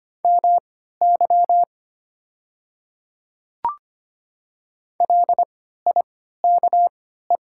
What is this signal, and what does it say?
Morse code, heard as a keyed tone, about 25 wpm, spaced as for 14 wpm, 700 Hz. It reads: MY LIKE
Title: 3x Character Spacing - Top 100 Words - 25wpm